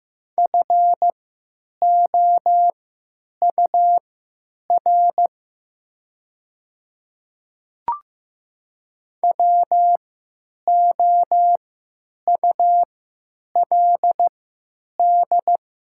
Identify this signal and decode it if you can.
Morse code, heard as a keyed tone, about 15 wpm, spaced as for 9 wpm, 700 Hz. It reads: FOUR WOULD
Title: 3x Character Spacing - Top 300 Words - 15wpm